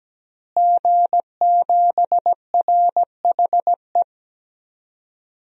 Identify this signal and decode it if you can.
Morse code, heard as a keyed tone, about 17 wpm, 700 Hz. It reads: G7RHE
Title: DXCC-100 Call Signs - 5 Characters - Repeated 3x - 17wpm